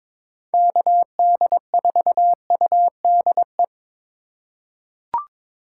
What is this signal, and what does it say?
Morse code, heard as a keyed tone, about 22 wpm, 700 Hz. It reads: KD4UDE
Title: USA Call Signs - 6 Characters - 2x Character Spacing - 22wpm